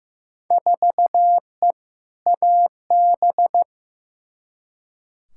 Morse code, heard as a keyed tone, about 15 wpm, 700 Hz. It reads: 4E AB
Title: Field Day Exchanges Encore - 15wpm